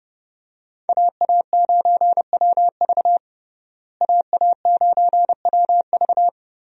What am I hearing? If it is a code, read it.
Morse code, heard as a keyed tone, about 30 wpm, 700 Hz. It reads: AA9WV AA9WV
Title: USA Call Signs - 5 Characters - Repeated 3x - 30wpm